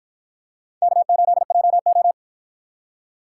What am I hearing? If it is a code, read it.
Morse code, heard as a keyed tone, about 53 wpm, 700 Hz. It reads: K8JO